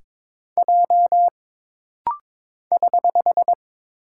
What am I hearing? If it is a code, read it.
Morse code, heard as a keyed tone, about 22 wpm, 700 Hz. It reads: J <HH>
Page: This rapid-fire timing is geared to push your ICR (Instant Character Recognition) skills.